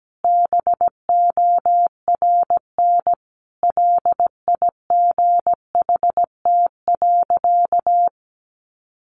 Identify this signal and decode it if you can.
Morse code, heard as a keyed tone, about 17 wpm, 700 Hz. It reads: BORN LIGHT.
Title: Sentences from Top 1100 Words 17wpm